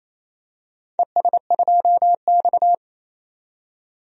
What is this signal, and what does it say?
Morse code, heard as a keyed tone, about 28 wpm, 700 Hz. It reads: ES2X
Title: DXCC-100 Call Signs - 4 Characters - 28wpm